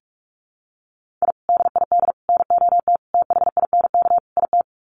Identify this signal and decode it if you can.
Morse code, heard as a keyed tone, about 44 wpm, 700 Hz. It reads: I DID NOT THINK IT
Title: Sets of 5 Words Encore2 45wpm